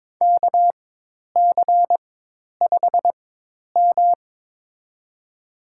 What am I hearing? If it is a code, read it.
Morse code, heard as a keyed tone, about 22 wpm, spaced as for 10 wpm, 700 Hz. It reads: KC5M
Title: USA Call Signs - 4 Characters - 4x Character Spacing - 22wpm